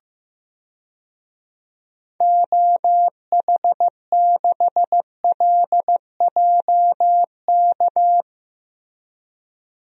Morse code, heard as a keyed tone, about 16 wpm, 700 Hz. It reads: OH6LJK
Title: DXCC-100 Call Signs - 6 Characters - Repeated 2x - 15wpm